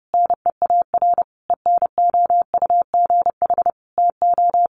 Morse code, heard as a keyed tone, about 30 wpm, 700 Hz. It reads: NEAR ENOUGH TO
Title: Sentences from Top 1200 Words 30wpm